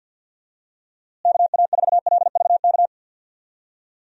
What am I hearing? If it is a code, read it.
Morse code, heard as a keyed tone, about 47 wpm, 700 Hz. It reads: KA4LVK
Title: USA Call Signs - 6 Characters - Repeated 3x - 50wpm